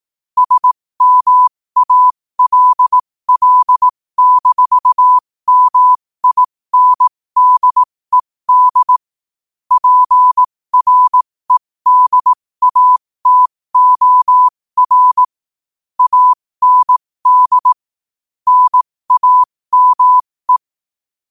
Morse code, heard as a keyed tone, about 18 wpm, spaced as for 15 wpm, 1000 Hz. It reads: SMALL-MINDED PREDATOR AND NAME
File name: qotdfiles / 15wpm